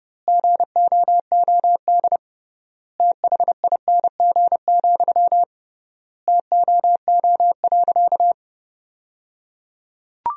Morse code, heard as a keyed tone, about 30 wpm, 700 Hz. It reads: GOOD THING, TOO.
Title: 3x Word-Spacing - Sentences from Top 200 Words - 30wpm